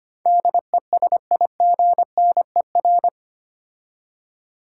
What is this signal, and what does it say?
Morse code, heard as a keyed tone, about 25 wpm, 700 Hz. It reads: DESIGNER
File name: 8 Letter Words - First 3 Letters Given - 25wpm